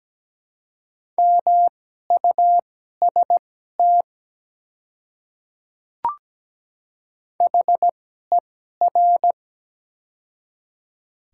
Morse code, heard as a keyed tone, about 17 wpm, spaced as for 12 wpm, 700 Hz. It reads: MUST HER
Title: 2x Character Spacing - Top 300 Words - 17wpm